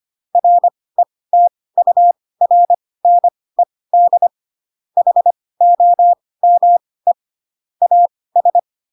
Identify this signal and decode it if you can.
Morse code, heard as a keyed tone, about 25 wpm, spaced as for 18 wpm, 700 Hz. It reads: RETURNED HOME AS
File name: Pinocchio / Pinocchio ch6 12 25 ik1hge